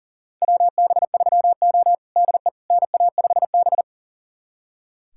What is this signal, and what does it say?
Morse code, heard as a keyed tone, about 40 wpm, 700 Hz. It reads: WB3O DE NA5B
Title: USA Callsign DE Callsign - 4 Characters - 40wpm